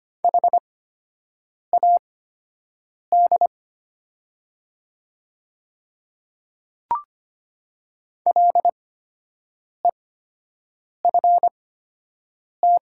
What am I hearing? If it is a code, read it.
Morse code, heard as a keyed tone, about 25 wpm, spaced as for 7 wpm, 700 Hz. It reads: HAD LEFT
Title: 8x Character Spacing - Top 300 Words - 25wpm